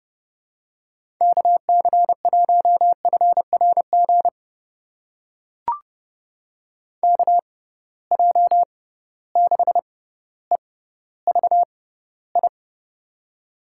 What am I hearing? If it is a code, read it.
Morse code, heard as a keyed tone, about 30 wpm, spaced as for 15 wpm, 700 Hz. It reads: KC1FRG KJ6EVI
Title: USA Call Signs - 6 Characters - 6x Character Spacing - 30wpm